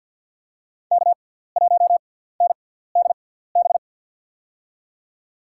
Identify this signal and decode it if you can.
Morse code, heard as a keyed tone, about 53 wpm, spaced as for 17 wpm, 700 Hz. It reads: K1NDB
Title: USA Call Signs - 5 Characters - 6x Character Spacing - 50wpm